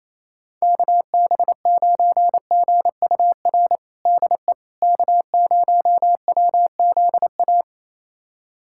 Morse code, heard as a keyed tone, about 28 wpm, 700 Hz. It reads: KB9GUR DE K0WZA
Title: USA Callsign DE Callsign 28wpm